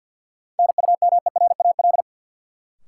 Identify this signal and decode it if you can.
Morse code, heard as a keyed tone, about 53 wpm, 700 Hz. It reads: NUMERAL
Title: Sentences from Top 1000 Words - Review Part 2 - New Word Drill - 50wpm